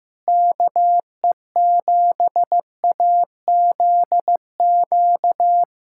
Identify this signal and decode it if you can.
Morse code, heard as a keyed tone, about 16 wpm, 700 Hz. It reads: KE7AZQ
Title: USA Callsign DE Callsign - 6 Characters - 15wpm